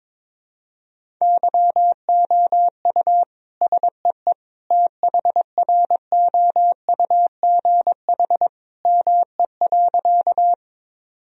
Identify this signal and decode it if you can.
Morse code, heard as a keyed tone, about 22 wpm, 700 Hz. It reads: YOU SEE THROUGH ME.
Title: Sentences from Top 1000 Words - Review Part 1 - 22wpm